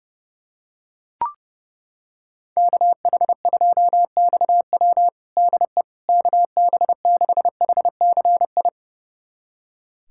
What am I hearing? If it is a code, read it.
Morse code, heard as a keyed tone, about 30 wpm, 700 Hz. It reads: KH2XW DE KB6HCI
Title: USA Callsign DE Callsign 30wpm